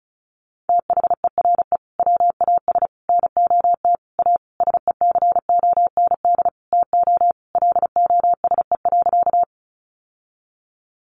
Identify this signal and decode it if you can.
Morse code, heard as a keyed tone, about 34 wpm, 700 Hz. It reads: THERE WAS NOT A SECOND TO LOSE.
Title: Sentences from Top 1100 Words 35wpm